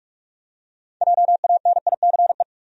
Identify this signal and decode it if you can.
Morse code, heard as a keyed tone, about 44 wpm, 700 Hz. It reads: JANICE
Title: USA Common Names - 45wpm